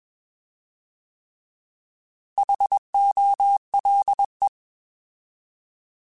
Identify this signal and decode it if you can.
Morse code, heard as a keyed tone, about 21 wpm, 800 Hz. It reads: HOLE